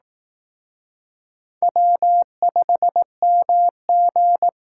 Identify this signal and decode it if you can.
Morse code, heard as a keyed tone, about 18 wpm, 700 Hz. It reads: W5MG
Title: Speed Racing - USA 1x2 Call Signs 15wpm